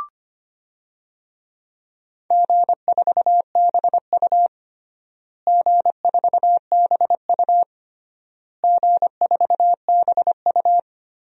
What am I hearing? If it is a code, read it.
Morse code, heard as a keyed tone, about 25 wpm, 700 Hz. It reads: G4BU G4BU G4BU
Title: DXCC-100 Call Signs - 4 Characters - Repeated 3x - 25wpm